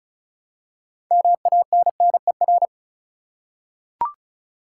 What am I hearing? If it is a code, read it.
Morse code, heard as a keyed tone, about 36 wpm, 700 Hz. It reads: MANNER
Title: Six Letter Words - 35wpm